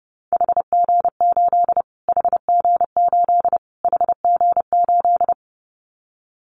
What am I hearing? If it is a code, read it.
Morse code, heard as a keyed tone, about 30 wpm, 700 Hz. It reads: HG8 HG8 HG8
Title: Callsign Prefixes for Top 25 DXCC - Repeated 3x - 30wpm